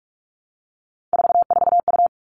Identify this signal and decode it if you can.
Morse code, heard as a keyed tone, about 44 wpm, 700 Hz. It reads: 44U
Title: All Cut Numbers - 3-Digit 45wpm